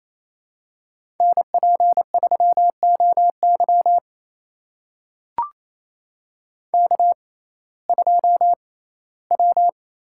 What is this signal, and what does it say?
Morse code, heard as a keyed tone, about 28 wpm, spaced as for 17 wpm, 700 Hz. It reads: NP3OY K2W
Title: USA Call Signs - 6x Character Spacing - 28wpm